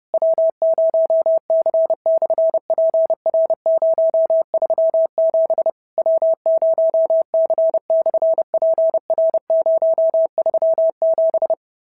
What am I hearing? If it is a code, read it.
Morse code, heard as a keyed tone, about 32 wpm, 650 Hz. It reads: W0C/PR037 W0C/PR037
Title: SOTA QSO Practice 30wpm